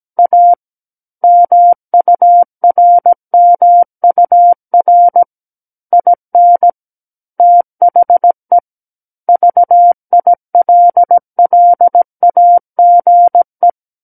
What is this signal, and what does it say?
Morse code, heard as a keyed tone, about 17 wpm, 700 Hz. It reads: A MURMUR IN THE VILLAGE
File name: Book One / 08-Chapter 8 - 17 WPM